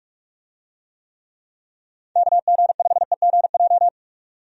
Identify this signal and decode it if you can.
Morse code, heard as a keyed tone, about 46 wpm, 700 Hz. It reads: KG5EGJ
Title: USA Call Signs - 6 Characters - 45wpm